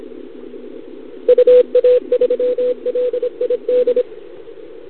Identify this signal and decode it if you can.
Morse code, heard as a keyed tone, about 25 wpm, 500 Hz. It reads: UA3LID
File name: CQWWCW2021 / Doderna OL3Z